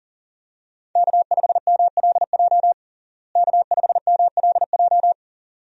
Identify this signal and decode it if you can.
Morse code, heard as a keyed tone, about 37 wpm, 700 Hz. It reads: K5MLJ K5MLJ
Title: USA Call Signs - 5 Characters - Repeated 2x - 40wpm